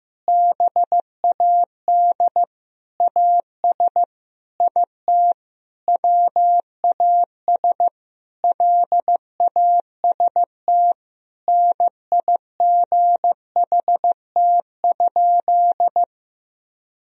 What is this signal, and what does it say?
Morse code, heard as a keyed tone, about 15 wpm, 700 Hz. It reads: BAD AS IT WAS LAST NIGHT?
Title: Sentences from Top 1000 Words - Review Part 1 - 15wpm